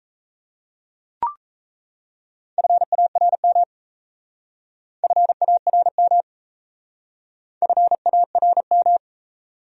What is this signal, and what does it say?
Morse code, heard as a keyed tone, about 35 wpm, 700 Hz. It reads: FARM FARM FARM